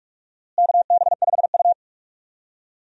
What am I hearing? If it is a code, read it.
Morse code, heard as a keyed tone, about 44 wpm, 700 Hz. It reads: KB5U
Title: USA Call Signs - 4 Characters - Repeated 3x - 45wpm